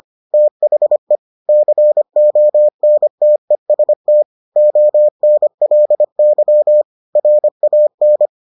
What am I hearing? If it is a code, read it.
Morse code, heard as a keyed tone, about 25 wpm, 600 Hz. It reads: THE CONTEST ONLY RAN